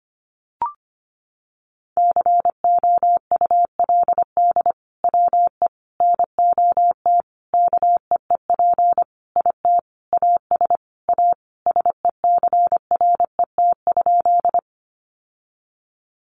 Morse code, heard as a keyed tone, about 25 wpm, 700 Hz. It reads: COULD WE NOT KEEP IT AS A SECRET?
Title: Sentences from Top 1200 Words 25wpm